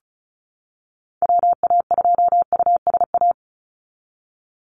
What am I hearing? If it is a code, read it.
Morse code, heard as a keyed tone, about 36 wpm, 700 Hz. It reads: WA2USA
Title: NA SOTA Activator Callsigns 35wpm